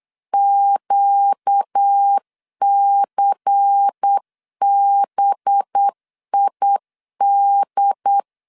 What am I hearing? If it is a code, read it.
Morse code, heard as a keyed tone, about 8 wpm, 800 Hz. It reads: QCBID
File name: LEZIONI / 6 PKX0 / 6 P 8wpm 3025